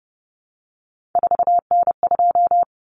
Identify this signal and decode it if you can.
Morse code, heard as a keyed tone, about 30 wpm, 700 Hz. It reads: 4N2
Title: Common Cut Numbers - 3-Digit 30wpm